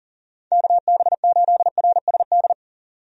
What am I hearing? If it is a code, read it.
Morse code, heard as a keyed tone, about 40 wpm, 700 Hz. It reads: KB8RSD
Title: USA Call Signs - 6 Characters - 40wpm